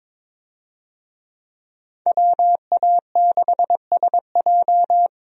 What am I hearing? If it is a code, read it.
Morse code, heard as a keyed tone, about 22 wpm, 700 Hz. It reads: WA6SJ
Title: USA Call Signs - 5 Characters - Repeated 2x - 22wpm